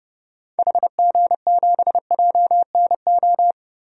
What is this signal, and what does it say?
Morse code, heard as a keyed tone, about 30 wpm, 700 Hz. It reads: HG7JNO
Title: DXCC-100 Call Signs - 6 Characters - Repeated 2x - 30wpm